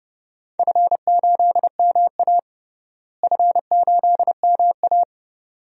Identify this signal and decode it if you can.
Morse code, heard as a keyed tone, about 30 wpm, 700 Hz. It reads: F8MA F8MA
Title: DXCC-100 Call Signs - 4 Characters - Repeated 2x - 30wpm